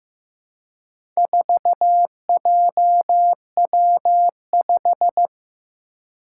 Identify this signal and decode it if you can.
Morse code, heard as a keyed tone, about 15 wpm, 700 Hz. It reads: 4JW5
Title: Four Challenging Letter-Number Combinations 15wpm